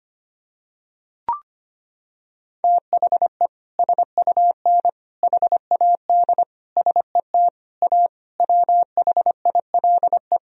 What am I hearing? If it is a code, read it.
Morse code, heard as a keyed tone, about 25 wpm, 700 Hz. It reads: THE SUN HAD SET A WHILE
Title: Sentences from Top 1000 Words - Review Part 1 - 25wpm